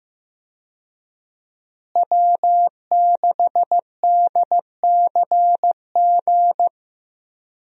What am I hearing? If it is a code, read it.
Morse code, heard as a keyed tone, about 15 wpm, 700 Hz. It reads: W6DCG
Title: USA Call Signs - 5 Characters - Repeated 2x - 15wpm